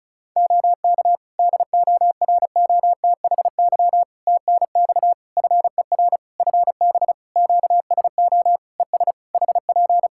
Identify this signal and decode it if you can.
Morse code, heard as a keyed tone, about 36 wpm, 700 Hz. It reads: OK DOROTHY TNX FER FB QSO ES HP
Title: Practice QSO - Exchange 4 - Ending - 35wpm